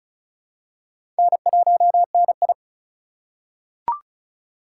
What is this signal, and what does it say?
Morse code, heard as a keyed tone, about 36 wpm, 700 Hz. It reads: N1NI
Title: USA Call Signs - 4 Characters - 2x Character Spacing - 35wpm